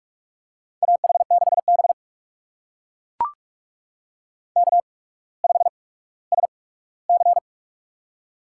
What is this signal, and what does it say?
Morse code, heard as a keyed tone, about 44 wpm, spaced as for 19 wpm, 700 Hz. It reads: AH6B K5SC
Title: USA Call Signs - 4 Characters - 8x Character Spacing - 45wpm